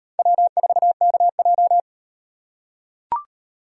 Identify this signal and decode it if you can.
Morse code, heard as a keyed tone, about 38 wpm, 700 Hz. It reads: W4KJ